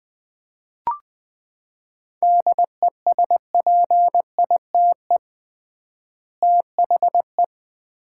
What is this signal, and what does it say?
Morse code, heard as a keyed tone, about 21 wpm, 700 Hz. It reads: DESPITE THE
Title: 3x Word-Spacing - Sets of 4 Words - 20wpm